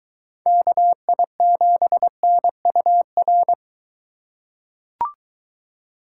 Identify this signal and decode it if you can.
Morse code, heard as a keyed tone, about 23 wpm, 700 Hz. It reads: KI7NUR